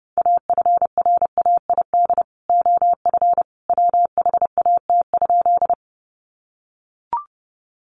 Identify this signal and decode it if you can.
Morse code, heard as a keyed tone, about 30 wpm, 700 Hz. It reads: AFRAID OF WHAT?
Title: Sentences from Top 1500 Words 30wpm